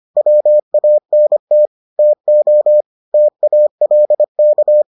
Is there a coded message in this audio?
Yes: Morse code from 0.2 to 4.8 s, heard as a keyed tone, about 25 wpm, 600 Hz. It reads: WANT TO TALK